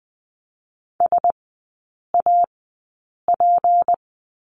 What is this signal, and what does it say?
Morse code, heard as a keyed tone, about 20 wpm, spaced as for 8 wpm, 700 Hz. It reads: SAP
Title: 152 Lesson 25e - Letter p - Three Characters 20wpm